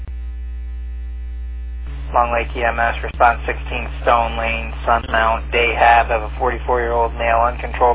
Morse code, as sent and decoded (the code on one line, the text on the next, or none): none